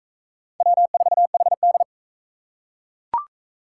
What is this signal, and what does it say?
Morse code, heard as a keyed tone, about 41 wpm, 700 Hz. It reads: W4HD